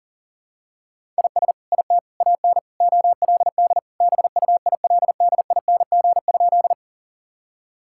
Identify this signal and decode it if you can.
Morse code, heard as a keyed tone, about 40 wpm, 700 Hz. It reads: IS IT AN OLD BUILDING?
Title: Sentences from Top 1000 Words - Review Part 5 - 40wpm